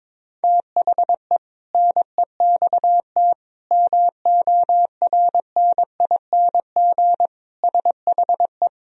Morse code, heard as a keyed tone, about 21 wpm, 700 Hz. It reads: THE NEXT MORNING SHE
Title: Sentences from Top 1000 Words - Review Part 2 - 22wpm